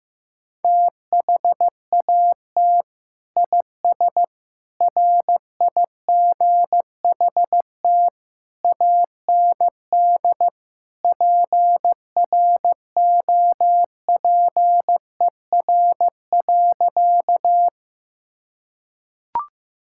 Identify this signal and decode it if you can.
Morse code, heard as a keyed tone, about 15 wpm, 700 Hz. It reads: THAT IS RIGHT AND PROPER.
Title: Sentences from Top 1000 Words - Review Part 5 - 15wpm